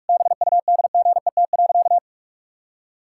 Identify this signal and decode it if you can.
Morse code, heard as a keyed tone, about 47 wpm, 700 Hz. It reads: BUDGET.
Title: Pangram Challenge v2 - 45wpm